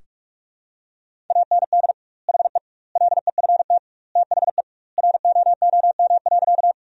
Morse code, heard as a keyed tone, about 47 wpm, 700 Hz. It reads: AND HE LEFT THE ROOM.
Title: 2x Word-Spacing - Sentences from Top 300 Words - 45wpm